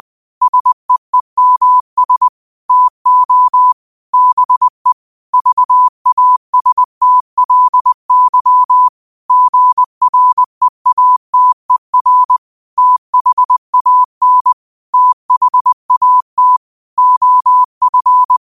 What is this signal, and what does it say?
Morse code, heard as a keyed tone, about 20 wpm, 1000 Hz. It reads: SEEMS TO BE VASTLY GREATER THAN THAT OF
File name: qotdfiles / 20wpm